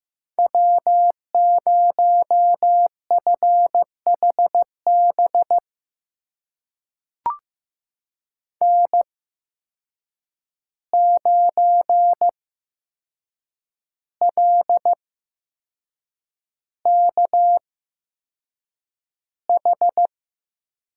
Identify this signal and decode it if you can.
Morse code, heard as a keyed tone, about 15 wpm, spaced as for 6 wpm, 700 Hz. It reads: W0FHB N9LKH
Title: USA Call Signs - 5 Characters - 8x Character Spacing - 15wpm